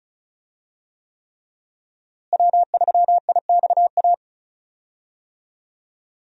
Morse code, heard as a keyed tone, about 34 wpm, 700 Hz. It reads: W3IXA
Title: USA Call Signs - 5 Characters - Repeated 3x - 35wpm